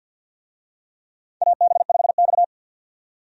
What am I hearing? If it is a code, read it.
Morse code, heard as a keyed tone, about 50 wpm, 700 Hz. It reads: AB5X
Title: USA Call Signs - 4 Characters - 50wpm